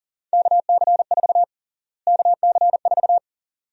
Morse code, heard as a keyed tone, about 40 wpm, 700 Hz. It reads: KC4 KC4
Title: USA Call Sign Prefixes - Repeated 2x - 40wpm